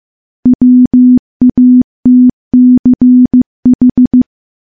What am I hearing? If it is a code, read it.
Morse code, heard as a keyed tone, about 15 wpm, 250 Hz. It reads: WATCH
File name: morse / morse-training-15WPM